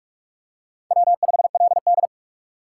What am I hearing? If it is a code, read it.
Morse code, heard as a keyed tone, about 46 wpm, 700 Hz. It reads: W5LD
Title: USA Call Signs - 6x Character Spacing - 45wpm